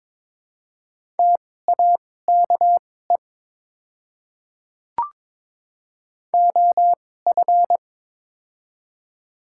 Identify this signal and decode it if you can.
Morse code, heard as a keyed tone, about 22 wpm, spaced as for 16 wpm, 700 Hz. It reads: TAKE OF
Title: 2x Character Spacing - Top 200 Words - 22wpm